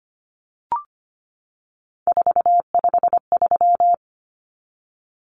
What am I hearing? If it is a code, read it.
Morse code, heard as a keyed tone, about 25 wpm, 700 Hz. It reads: 453
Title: Common Cut Numbers - 3-Digit 25wpm